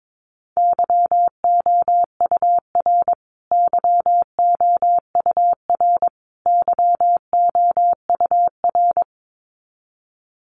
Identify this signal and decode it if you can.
Morse code, heard as a keyed tone, about 22 wpm, 700 Hz. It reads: YOUR YOUR YOUR
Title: Top 500 Words Repeated 22wpm